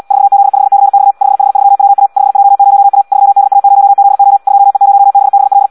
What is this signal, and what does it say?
Morse code, heard as a keyed tone, about 74 wpm, 800 Hz. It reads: 4Z5CX 5H3WA 5J0A 3DA0ZO 2E0BBL
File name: Audio practica CW / qrq-tx-78 2